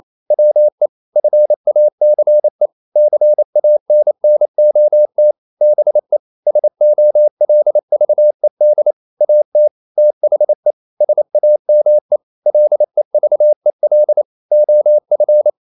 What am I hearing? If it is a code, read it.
Morse code, heard as a keyed tone, about 28 wpm, 600 Hz. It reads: WE FACE CANNOT BE SOLVED AT THE SAME LEVEL OF
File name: CW / Practice / EnglishQuotations0228